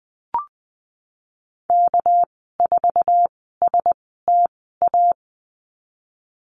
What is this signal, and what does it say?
Morse code, heard as a keyed tone, about 20 wpm, spaced as for 15 wpm, 700 Hz. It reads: K4STA
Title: 207 Lesson 33g - Letter j - Call Signs 20wpm